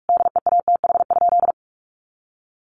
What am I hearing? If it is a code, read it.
Morse code, heard as a keyed tone, about 44 wpm, 700 Hz. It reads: DEATH?
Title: Sentences from Top 1100 Words 45wpm